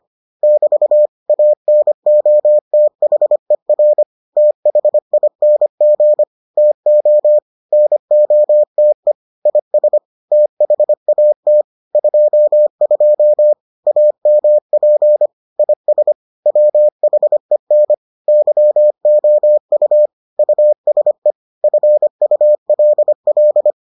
Morse code, heard as a keyed tone, about 25 wpm, 600 Hz. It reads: <BT> ANOTHER THING TO NOTE IS THAT 22 AMP IS WHEN YOU USE FULL